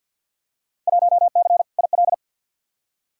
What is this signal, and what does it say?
Morse code, heard as a keyed tone, about 47 wpm, 700 Hz. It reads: 1C IL
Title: Field Day Exchanges - 50wpm